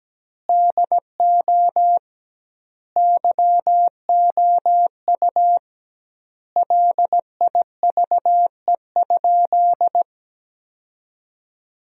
Morse code, heard as a keyed tone, about 17 wpm, 700 Hz. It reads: DO YOU LIVE?
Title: 2x Word-Spacing - Sentences from Top 200 Words - 17wpm